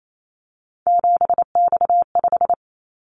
Morse code, heard as a keyed tone, about 28 wpm, 700 Hz. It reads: 7X5
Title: Callsign Prefixes for Top 100 DXCC - 28wpm